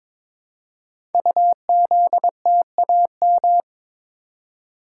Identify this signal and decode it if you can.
Morse code, heard as a keyed tone, about 22 wpm, 700 Hz. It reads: UZTAM
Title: Code Groups - 5 Letters - 22wpm